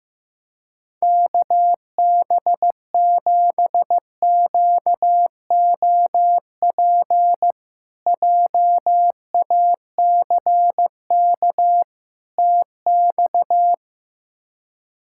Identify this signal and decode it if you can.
Morse code, heard as a keyed tone, about 15 wpm, 700 Hz. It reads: KB7QOP JACK TX
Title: K1USN SST QSO Practice - 15wpm